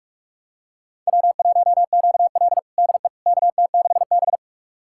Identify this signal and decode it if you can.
Morse code, heard as a keyed tone, about 46 wpm, 700 Hz. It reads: W1QL DE KT6B
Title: USA Callsign DE Callsign - 4 Characters - 45wpm